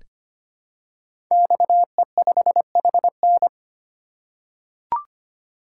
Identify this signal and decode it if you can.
Morse code, heard as a keyed tone, about 25 wpm, 700 Hz. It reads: XE5HN
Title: Code Groups - 5 Letter:Numbers - 25wpm